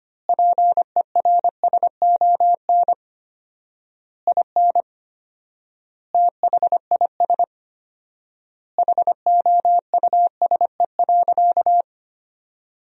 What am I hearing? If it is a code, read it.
Morse code, heard as a keyed tone, about 25 wpm, 700 Hz. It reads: PERSON IN THIS HOUSE.
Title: 4x Word-Spacing - Sentences from Top 400 Words - 25wpm